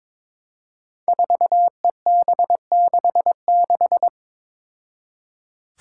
Morse code, heard as a keyed tone, about 22 wpm, 700 Hz. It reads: 4EB66
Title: Five Challenging Letter-Number Combinations 22wpm